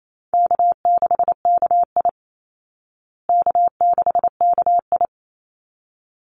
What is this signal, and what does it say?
Morse code, heard as a keyed tone, about 28 wpm, 700 Hz. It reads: K6KI K6KI
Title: Easy Call Signs Repeated 2x - 28wpm